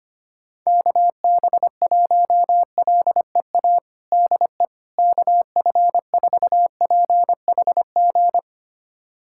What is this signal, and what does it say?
Morse code, heard as a keyed tone, about 25 wpm, 700 Hz. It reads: KB1LEA DE KF4PHG
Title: USA Callsign DE Callsign 25wpm